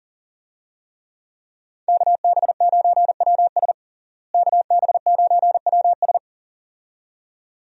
Morse code, heard as a keyed tone, about 40 wpm, 700 Hz. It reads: KB9WS KB9WS
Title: USA Call Signs - 5 Characters - Repeated 2x - 40wpm